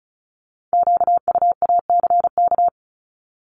Morse code, heard as a keyed tone, about 36 wpm, 700 Hz. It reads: QUACK
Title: Multi-letter Phonemes 35wpm